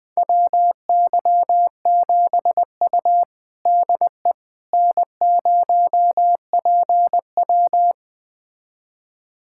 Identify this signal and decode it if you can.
Morse code, heard as a keyed tone, about 21 wpm, 700 Hz. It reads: WY7U DE N0PW
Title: USA Callsign DE Callsign - 4 Characters - 20wpm